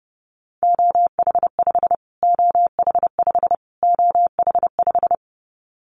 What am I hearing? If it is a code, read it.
Morse code, heard as a keyed tone, about 30 wpm, 690 Hz. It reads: OH5 OH5 OH5
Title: Callsign Prefixes for Top 25 DXCC - Repeated 3x - 30wpm